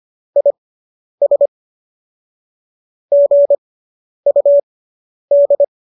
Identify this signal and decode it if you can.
Morse code, heard as a keyed tone, about 25 wpm, spaced as for 10 wpm, 570 Hz. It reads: IS GUD